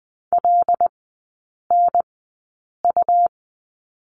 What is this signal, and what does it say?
Morse code, heard as a keyed tone, about 20 wpm, spaced as for 8 wpm, 700 Hz. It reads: LNU
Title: 133 Lesson 22e - Question Mark - Three Characters 20wpm